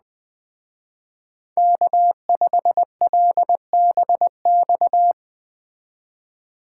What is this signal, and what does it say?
Morse code, heard as a keyed tone, about 19 wpm, 700 Hz. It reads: K5LBX
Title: Speed Racing - USA 1x3 Call Signs 17wpm